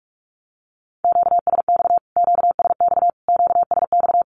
Morse code, heard as a keyed tone, about 44 wpm, 700 Hz. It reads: QSX QSX QSX
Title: Q-Codes Repeated 3x 45wpm